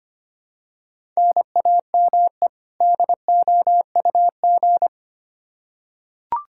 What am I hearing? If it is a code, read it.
Morse code, heard as a keyed tone, about 25 wpm, 700 Hz. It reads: NAME DOUG
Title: Practice QSO - Exchange 1 - Beginning - 25wpm